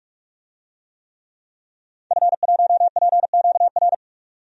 Morse code, heard as a keyed tone, about 44 wpm, 700 Hz. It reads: F1PQR
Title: DXCC-100 Call Signs - 5 Characters - 45wpm